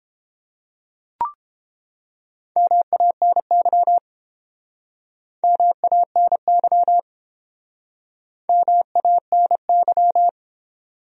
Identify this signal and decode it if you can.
Morse code, heard as a keyed tone, about 30 wpm, 700 Hz. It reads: MANY MANY MANY